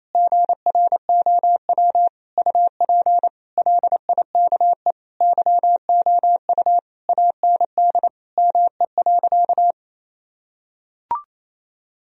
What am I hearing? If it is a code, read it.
Morse code, heard as a keyed tone, about 28 wpm, 700 Hz. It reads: GROW UP LIKE YOU AND ME.
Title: Sentences from Top 1000 Words - Review Part 1 - 28wpm